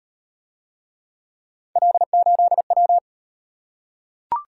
Morse code, heard as a keyed tone, about 38 wpm, 700 Hz. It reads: L8W